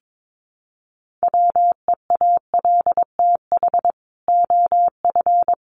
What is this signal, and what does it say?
Morse code, heard as a keyed tone, about 22 wpm, 700 Hz. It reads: WEALTH OF
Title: Sentences from Top 1200 Words 22wpm